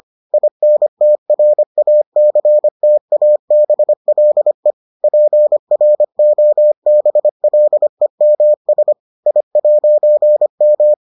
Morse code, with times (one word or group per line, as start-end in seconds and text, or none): 0.3-4.7 INTRACTABLE
5.0-8.9 PROBLEMS
9.3-10.9 I'M